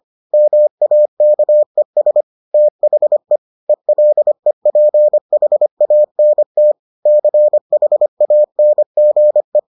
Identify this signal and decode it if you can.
Morse code, heard as a keyed tone, about 25 wpm, 600 Hz. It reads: MAKES THE ELEPHANT CHANGE